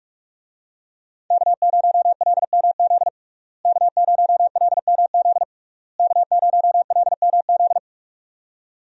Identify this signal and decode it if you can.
Morse code, heard as a keyed tone, about 46 wpm, 700 Hz. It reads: K0LMZ K0LMZ K0LMZ
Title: USA Call Signs - 5 Characters - Repeated 3x - 45wpm